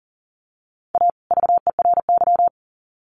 Morse code, heard as a keyed tone, about 40 wpm, 700 Hz. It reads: A VERY
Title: Sets of 2 Words 40wpm